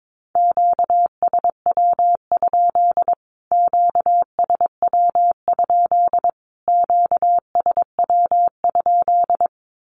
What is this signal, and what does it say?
Morse code, heard as a keyed tone, about 22 wpm, 700 Hz. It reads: QSW? QSW? QSW?
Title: Q-Codes Repeated 3x 22wpm